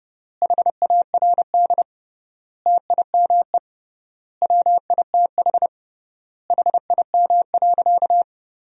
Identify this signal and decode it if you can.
Morse code, heard as a keyed tone, about 30 wpm, 700 Hz. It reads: HARD TIME WITH HIM.
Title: 3x Word-Spacing - Sentences from Top 300 Words - 30wpm